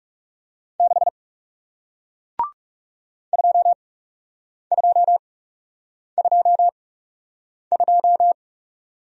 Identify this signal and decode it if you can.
Morse code, heard as a keyed tone, about 39 wpm, spaced as for 8 wpm, 700 Hz. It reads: 6 2222